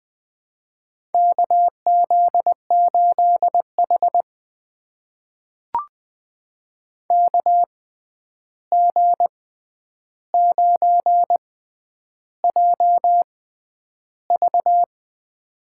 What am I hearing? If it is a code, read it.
Morse code, heard as a keyed tone, about 20 wpm, spaced as for 10 wpm, 700 Hz. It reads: KZ8H KG9JV
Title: USA Call Signs - 6x Character Spacing - 20wpm